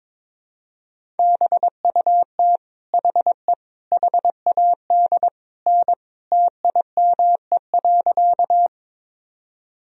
Morse code, heard as a keyed tone, about 22 wpm, 700 Hz. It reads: BUT HE HAD N TIME.
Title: Sentences from Top 100 Words and Abbreviations 22wpm